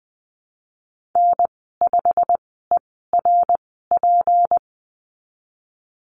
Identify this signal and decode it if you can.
Morse code, heard as a keyed tone, about 20 wpm, spaced as for 15 wpm, 700 Hz. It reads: N5ERP
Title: 154 Lesson 25g - Letter p - Call Signs 20wpm